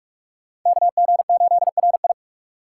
Morse code, heard as a keyed tone, about 38 wpm, 700 Hz. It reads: KG8RI